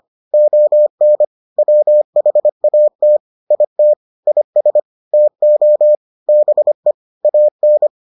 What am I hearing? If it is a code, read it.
Morse code, heard as a keyed tone, about 25 wpm, 600 Hz. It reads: ON WHAT IT IS TO BE AN